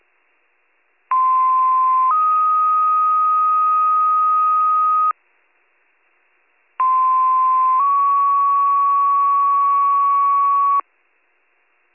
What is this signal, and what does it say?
Two-tone page, heard as a one-second tone, then a three-second tone, about 1040 then 1250 Hz, about 1040 then 1120 Hz. It is heard from 1.1 to 10.8 s.